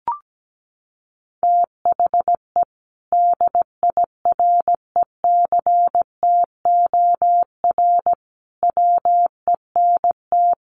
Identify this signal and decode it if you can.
Morse code, heard as a keyed tone, about 17 wpm, 700 Hz. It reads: THE DIRECTOR WENT
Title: Sentences from Top 1300 Words 17wpm